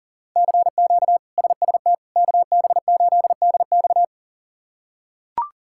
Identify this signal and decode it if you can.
Morse code, heard as a keyed tone, about 40 wpm, 700 Hz. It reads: CQ SST KB8DX
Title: K1USN SST QSO Practice - 40wpm